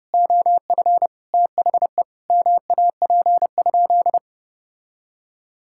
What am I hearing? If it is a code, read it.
Morse code, heard as a keyed tone, about 32 wpm, 700 Hz. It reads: OF THE MAP?
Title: Sentences from Top 1000 Words - Review Part 5 - 30wpm